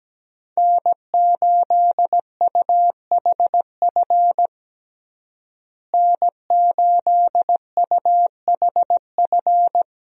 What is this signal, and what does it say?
Morse code, heard as a keyed tone, about 17 wpm, 700 Hz. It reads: N8UHF N8UHF
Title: USA Call Signs - 5 Characters - Repeated 2x - 17wpm